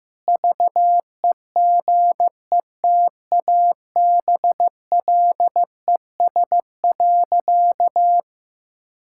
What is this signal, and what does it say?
Morse code, heard as a keyed tone, about 14 wpm, 700 Hz. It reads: VEGETABLES.
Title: Pangram Challenge v2 - 15wpm